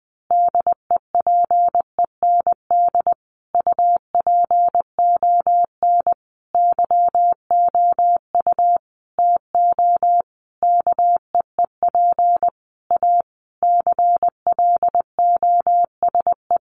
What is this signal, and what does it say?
Morse code, heard as a keyed tone, about 20 wpm, 700 Hz. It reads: DEPEND UPON YOU TO KEEP A CLOSE
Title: Sentences from Top 1300 Words 20wpm